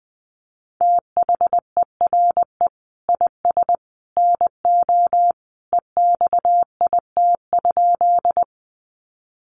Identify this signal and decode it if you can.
Morse code, heard as a keyed tone, about 20 wpm, 700 Hz. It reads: THERE IS NO EXIT?
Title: Sentences from Top 1100 Words 20wpm